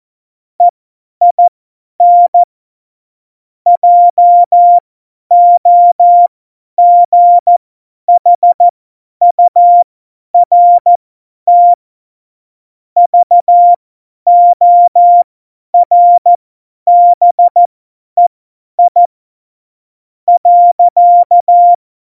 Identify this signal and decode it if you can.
Morse code, heard as a keyed tone, about 14 wpm, spaced as for 10 wpm, 700 Hz. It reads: EIN JOGHURT VORBEI .
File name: Morsekurs / Witze / Witz 29. Mai 2018, 50 F70